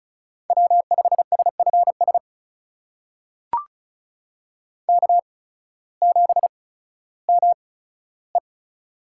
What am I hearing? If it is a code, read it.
Morse code, heard as a keyed tone, about 36 wpm, spaced as for 16 wpm, 700 Hz. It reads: W5SFS K7ME
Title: USA Call Signs - 8x Character Spacing - 35wpm